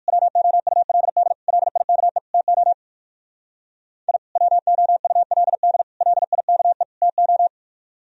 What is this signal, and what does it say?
Morse code, heard as a keyed tone, about 62 wpm, 700 Hz. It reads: WOULD LIKE TO I WOULD LIKE TO